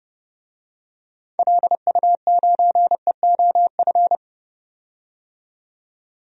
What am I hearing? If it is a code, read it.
Morse code, heard as a keyed tone, about 31 wpm, 700 Hz. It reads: LU9EOF
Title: DXCC-100 Call Signs - 6 Characters - 30wpm